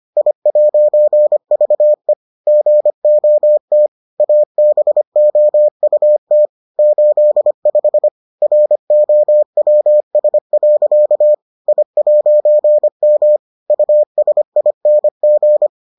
Morse code, heard as a keyed tone, about 25 wpm, 600 Hz. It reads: I'VE GOT ABOUT 85 ROWS. I'M USING